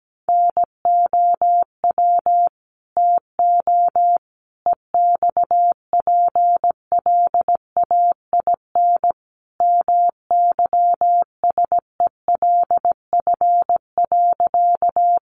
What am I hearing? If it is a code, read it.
Morse code, heard as a keyed tone, about 17 wpm, 700 Hz. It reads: NOW TO EXPLAIN MYSELF.
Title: Sentences from Top 1300 Words 17wpm